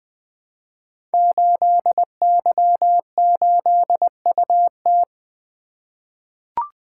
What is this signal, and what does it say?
Morse code, heard as a keyed tone, about 20 wpm, 700 Hz. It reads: 8Y8UT
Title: Code Groups - 5 Letter:Numbers - 20wpm